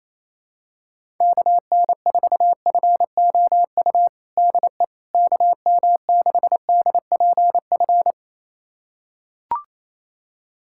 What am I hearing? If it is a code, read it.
Morse code, heard as a keyed tone, about 28 wpm, 700 Hz. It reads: KN4FOU DE KM6DPF
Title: USA Callsign DE Callsign - 6 Characters - 28wpm